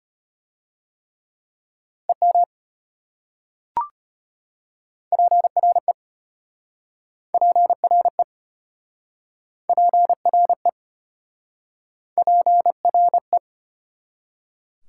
Morse code, heard as a keyed tone, about 33 wpm, 700 Hz. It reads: EM PRE PRE PRE PRE